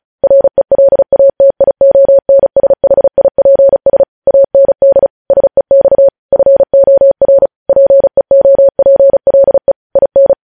Morse code, heard as a keyed tone, about 36 wpm, 550 Hz. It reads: RELATIONSHIPS AND SEX FOR PEOPLE IN
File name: podcast / News headlines-1x-35x35